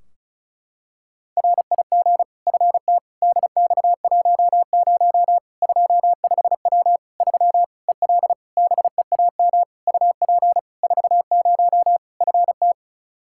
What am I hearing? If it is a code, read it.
Morse code, heard as a keyed tone, about 36 wpm, 700 Hz. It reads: RIG FT DX10 25W 3 EL BEAM UP 40 FT
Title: Practice QSO - Exchange 2 - Info - 35wpm